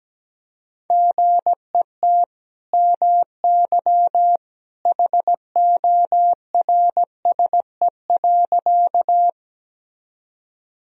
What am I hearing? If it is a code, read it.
Morse code, heard as a keyed tone, about 17 wpm, 700 Hz. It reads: GET MY HORSE.
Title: Sentences from Top 1000 Words - Review Part 1 - 17wpm